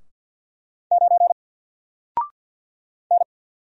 Morse code, heard as a keyed tone, about 50 wpm, 700 Hz. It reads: 9 N
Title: Single Character (Letter:Number:Punctuation:Prosign) 50wpm